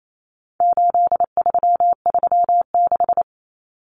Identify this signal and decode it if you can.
Morse code, heard as a keyed tone, about 28 wpm, 700 Hz. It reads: 8336
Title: Common Cut Numbers - 4-Digit 28wpm